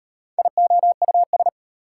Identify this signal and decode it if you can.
Morse code, heard as a keyed tone, about 38 wpm, 690 Hz. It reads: IOUS